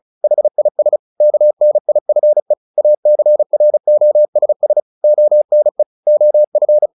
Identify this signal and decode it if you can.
Morse code, heard as a keyed tone, about 34 wpm, 600 Hz. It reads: HIS KNIFE ACROSS ONE OF
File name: the old man and the sea / the old man and the sea 1 35wpm